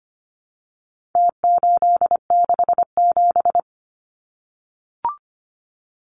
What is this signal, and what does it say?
Morse code, heard as a keyed tone, about 25 wpm, 700 Hz. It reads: T867
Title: Numbers - Contest Style - 4-Digit 25wpm